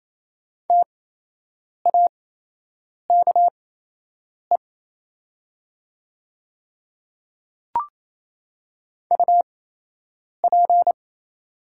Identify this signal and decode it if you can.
Morse code, heard as a keyed tone, about 27 wpm, spaced as for 8 wpm, 700 Hz. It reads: TAKE UP
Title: 8x Character Spacing - Top 300 Words - 28wpm